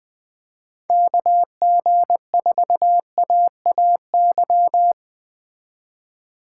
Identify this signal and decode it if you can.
Morse code, heard as a keyed tone, about 20 wpm, 700 Hz. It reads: KG4AAY
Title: Speed Racing - USA 2x3 Call Signs 17wpm